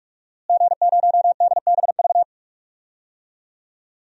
Morse code, heard as a keyed tone, about 46 wpm, 700 Hz. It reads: G0DBV
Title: DXCC-100 Call Signs - 5 Characters - Repeated 2x - 45wpm